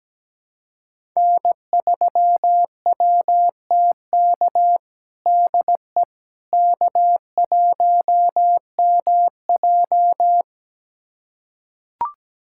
Morse code, heard as a keyed tone, about 17 wpm, 700 Hz. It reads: N3WTK DE K1MJ
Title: USA Callsign DE Callsign 17wpm